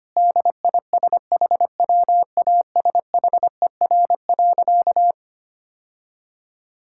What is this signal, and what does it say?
Morse code, heard as a keyed tone, about 25 wpm, 700 Hz. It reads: DISHWASHER.
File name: Easy Sentences with Hints - 25wpm